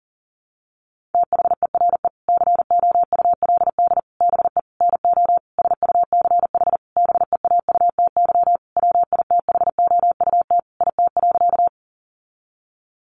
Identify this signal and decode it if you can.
Morse code, heard as a keyed tone, about 40 wpm, 700 Hz. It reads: THERE COULD BE NO SUCH BEAUTY WITHOUT IT.
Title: Sentences from Top 600 Words 40wpm